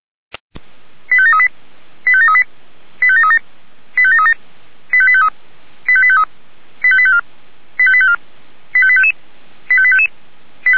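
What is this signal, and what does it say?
Five-tone selective call (ZVEI), heard as quick runs of five tones, tones of about 70 ms. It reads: 85738 85738 85737 85737 85853 85853 85854 85854 85855 85855
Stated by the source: ZVEI five-tone call received by SDR.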